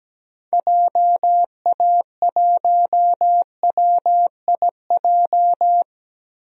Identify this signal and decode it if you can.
Morse code, heard as a keyed tone, about 17 wpm, 700 Hz. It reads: JA1WIJ
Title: DXCC-100 Call Signs - 6 Characters - Repeated 2x - 17wpm